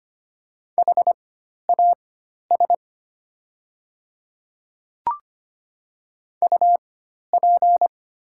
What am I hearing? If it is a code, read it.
Morse code, heard as a keyed tone, about 25 wpm, spaced as for 12 wpm, 700 Hz. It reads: HAS UP
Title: 4x Character Spacing - Top 100 Words - 25wpm